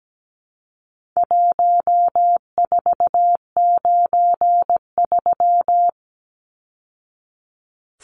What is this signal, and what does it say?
Morse code, heard as a keyed tone, about 17 wpm, 700 Hz. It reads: 1493
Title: Numbers - Contest Style - 4-Digit 17wpm